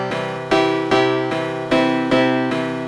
There is no code.